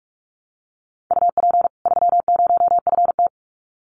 Morse code, heard as a keyed tone, about 44 wpm, 700 Hz. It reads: UP 30FT
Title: Common QSO Phrases 45wpm